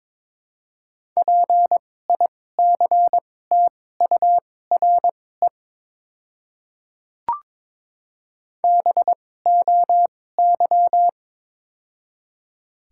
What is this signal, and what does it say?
Morse code, heard as a keyed tone, about 22 wpm, spaced as for 16 wpm, 700 Hz. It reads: PICTURE BOY
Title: 2x Character Spacing - Top 300 Words - 22wpm